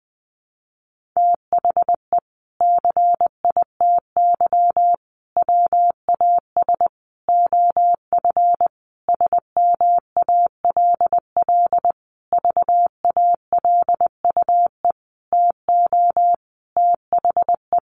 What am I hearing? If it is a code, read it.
Morse code, heard as a keyed tone, about 20 wpm, 700 Hz. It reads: THE CITY WAS OF SMALL VALUE TO THE
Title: Sentences from Top 1500 Words 20wpm